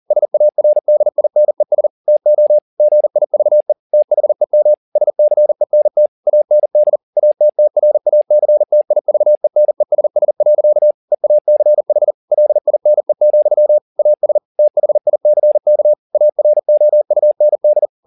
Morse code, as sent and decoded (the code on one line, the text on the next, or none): ... .- .-. -.. .. -. ... / - --- / --. .. ...- . / - .... -- / ... -.-. -. - / .- -. -.. / .- - - .-. .- -.-. - .. ...- . -. ... ... .-.-.- / . .- -.-. .... / .-.. .. -. --..-- / .- ... / - .... .. -.-. -.- / .- .-. --- ..- -. -..
SARDINES TO GIVE THEM SCENT AND ATTRACTIVENESS. EACH LINE, AS THICK AROUND